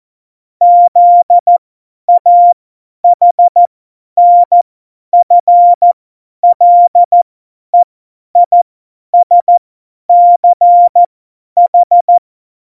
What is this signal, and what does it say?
Morse code, heard as a keyed tone, about 14 wpm, spaced as for 10 wpm, 700 Hz. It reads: ZAHNFLEISCH
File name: Morsekurs / Witze / Witz 02. Januar 2018, 50 F70